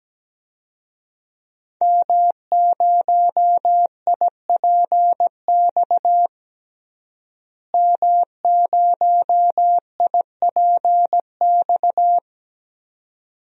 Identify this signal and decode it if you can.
Morse code, heard as a keyed tone, about 17 wpm, 700 Hz. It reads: M0IPX M0IPX
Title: DXCC-100 Call Signs - 5 Characters - Repeated 2x - 17wpm